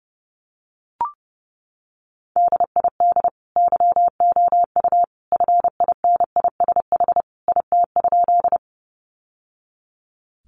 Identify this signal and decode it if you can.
Morse code, heard as a keyed tone, about 30 wpm, 700 Hz. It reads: DID YOU FINISH IT?
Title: Sentences from Top 1200 Words 30wpm